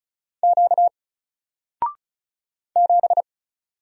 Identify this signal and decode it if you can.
Morse code, heard as a keyed tone, about 35 wpm, 700 Hz. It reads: Q 7
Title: Single Letter-Number 35wpm